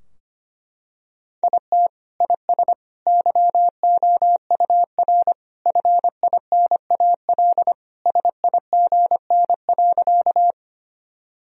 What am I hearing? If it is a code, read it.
Morse code, heard as a keyed tone, about 25 wpm, 700 Hz. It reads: IT IS YOUR FINAL SIGN.
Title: Sentences from Top 1000 Words - Review Part 2 - 25wpm